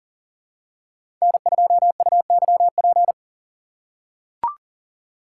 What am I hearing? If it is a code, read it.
Morse code, heard as a keyed tone, about 40 wpm, 700 Hz. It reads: N2UYP
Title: USA Call Signs - 5 Characters - 6x Character Spacing - 40wpm